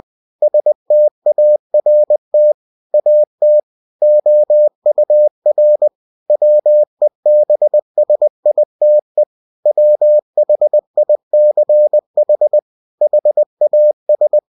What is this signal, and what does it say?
Morse code, heard as a keyed tone, about 20 wpm, 600 Hz. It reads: START AT OUR WEBSITE WHICH HAS
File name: cw-books / winnie-the-pooh / winnie-20wpm-page-0122